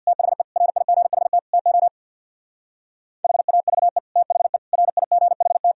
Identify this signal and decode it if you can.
Morse code, heard as a keyed tone, about 64 wpm, 700 Hz. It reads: THE RIGHT TO HAVE THE RIGHT